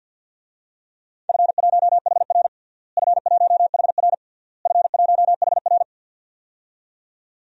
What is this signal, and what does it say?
Morse code, heard as a keyed tone, about 50 wpm, 700 Hz. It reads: F1HR F1HR F1HR
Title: DXCC-100 Call Signs - 4 Characters - Repeated 3x - 50wpm